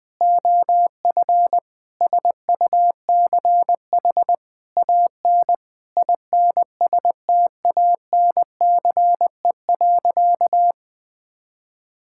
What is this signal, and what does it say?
Morse code, heard as a keyed tone, about 20 wpm, 700 Hz. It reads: OF SUCH AN INSTANCE.
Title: Sentences from Top 1000 Words - Review Part 5 - 20wpm